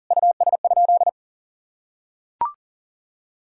Morse code, heard as a keyed tone, about 40 wpm, 700 Hz. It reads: US?
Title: Sentences from Top 1600 Words 40wpm